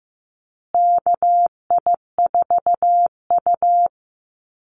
Morse code, heard as a keyed tone, about 15 wpm, 700 Hz. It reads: KI4U
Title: USA Call Signs Repeated 3x 15wpm